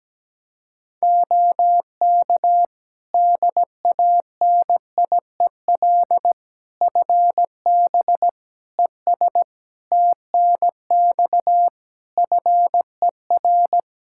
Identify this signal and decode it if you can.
Morse code, heard as a keyed tone, about 17 wpm, 700 Hz. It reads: OK DANIEL FB ES TNX FER
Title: Practice QSO - Exchange 2 - Info - 17wpm